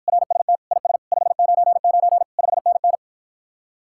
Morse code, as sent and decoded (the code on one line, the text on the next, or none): .-. ... - / .. ... / ..... ----. ----. / ..... -. -.
RST IS 599 5NN